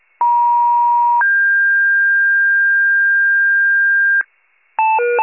Two-tone page, heard as a one-second tone, then a three-second tone, about 950 then 1600 Hz, with noise. It is heard from 0.2 to 4.2 s.